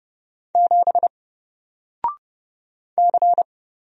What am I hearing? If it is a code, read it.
Morse code, heard as a keyed tone, about 30 wpm, 700 Hz. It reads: 7 C
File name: Single-Letter-Number-30wpm